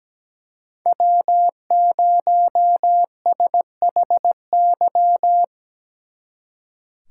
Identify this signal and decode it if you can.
Morse code, heard as a keyed tone, about 20 wpm, 700 Hz. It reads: W0SHY
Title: Speed Racing - USA 1x3 Call Signs 17wpm